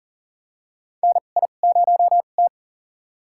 Morse code, heard as a keyed tone, about 41 wpm, spaced as for 29 wpm, 700 Hz. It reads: NI0T
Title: USA Call Signs - 4 Characters - 2x Character Spacing - 40wpm